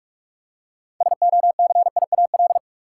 Morse code, heard as a keyed tone, about 42 wpm, 700 Hz. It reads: SOCIAL
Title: Practice QSO - Exchange 3 - More Info - 45wpm